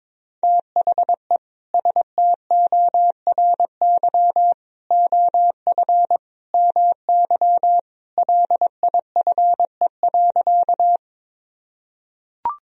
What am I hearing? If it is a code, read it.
Morse code, heard as a keyed tone, about 22 wpm, 700 Hz. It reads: THE STORY OF MY LIFE.
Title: Sentences from Top 1000 Words - Review Part 1 - 22wpm